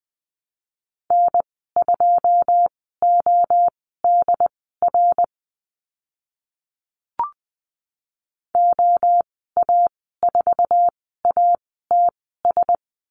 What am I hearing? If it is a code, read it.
Morse code, heard as a keyed tone, about 20 wpm, spaced as for 14 wpm, 700 Hz. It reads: N2ODR OA4ATS
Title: 082 Lesson 14g - Number 2 - Call Signs 20wpm